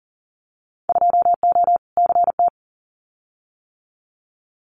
Morse code, heard as a keyed tone, about 40 wpm, 700 Hz. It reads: 2O CT
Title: Winter Field Day Exchanges 40wpm